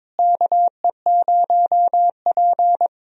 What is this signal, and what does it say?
Morse code, heard as a keyed tone, about 21 wpm, 700 Hz. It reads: KE0P
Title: USA Call Signs - 4 Characters - Repeated 2x - 22wpm